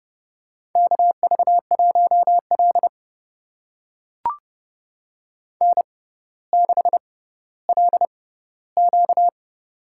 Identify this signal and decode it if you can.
Morse code, heard as a keyed tone, about 29 wpm, spaced as for 15 wpm, 700 Hz. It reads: KV1L N6LQ
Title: USA Call Signs - 4 Characters - 6x Character Spacing - 30wpm